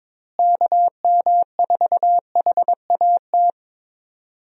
Morse code, heard as a keyed tone, about 22 wpm, 700 Hz. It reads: KM4HAT
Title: USA Call Signs - 6 Characters - Repeated 2x - 22wpm